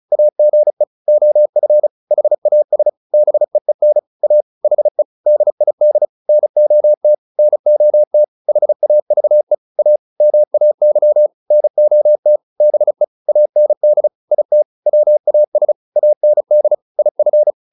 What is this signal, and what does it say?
Morse code, heard as a keyed tone, about 34 wpm, 600 Hz. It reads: AGE OF HAS BEEN A HE DID NOT NOT HAVE A MAY NOT BE AND IT WAS AND IF